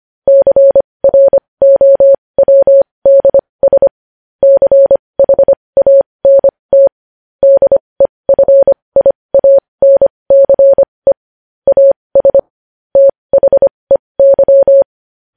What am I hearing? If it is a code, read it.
Morse code, heard as a keyed tone, about 25 wpm, spaced as for 20 wpm, 550 Hz. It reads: CROWDS CHANT DEFIANCE AS THEY
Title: News Headlines at 20